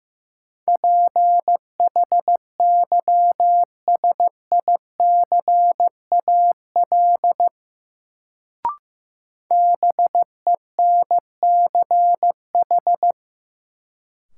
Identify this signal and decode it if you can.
Morse code, heard as a keyed tone, about 16 wpm, 700 Hz. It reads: PHYSICAL BENCH
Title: Sentences from Top 1400 Words - New Word Drill Alt - 15wpm